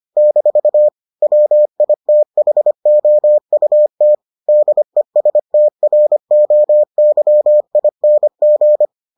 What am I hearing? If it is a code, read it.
Morse code, heard as a keyed tone, about 25 wpm, 600 Hz. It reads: - WITHOUT DESTROYING